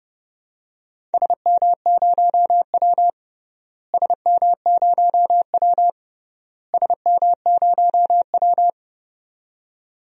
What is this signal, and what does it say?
Morse code, heard as a keyed tone, about 30 wpm, 700 Hz. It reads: SM0W SM0W SM0W
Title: DXCC-100 Call Signs - 4 Characters - Repeated 3x - 30wpm